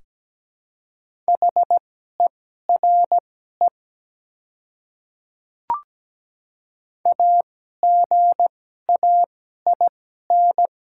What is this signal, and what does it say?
Morse code, heard as a keyed tone, about 17 wpm, spaced as for 12 wpm, 700 Hz. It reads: HERE AGAIN
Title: 2x Character Spacing - Top 200 Words - 17wpm